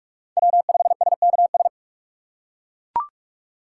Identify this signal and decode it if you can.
Morse code, heard as a keyed tone, about 40 wpm, 700 Hz. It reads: W5SKS